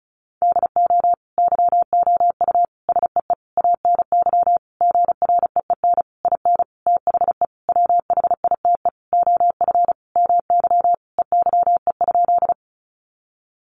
Morse code, heard as a keyed tone, about 34 wpm, 700 Hz. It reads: DO YOU SEE ANY GREEN IN THE WHITE OF MY EYE?
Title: Sentences from Top 500 Words 35wpm